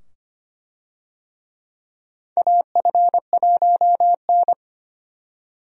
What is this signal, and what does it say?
Morse code, heard as a keyed tone, about 25 wpm, 700 Hz. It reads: AF1N
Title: USA Call Signs - 4 Characters - 25wpm